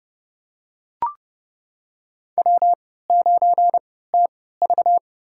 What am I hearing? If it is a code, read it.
Morse code, heard as a keyed tone, about 32 wpm, spaced as for 17 wpm, 700 Hz. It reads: W9TV
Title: USA Call Signs - 4 Characters - 3x Character Spacing - 30wpm